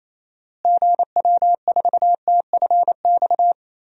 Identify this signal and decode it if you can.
Morse code, heard as a keyed tone, about 27 wpm, 700 Hz. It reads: GW4TFX
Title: DXCC-100 Call Signs - 6 Characters - Repeated 2x - 28wpm